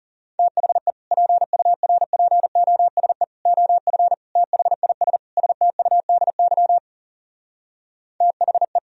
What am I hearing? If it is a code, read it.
Morse code, heard as a keyed tone, about 37 wpm, 700 Hz. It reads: THE PURPOSE OF THIS STUDY THE
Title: Speed-Racing - Sets of 5 Words 30wpm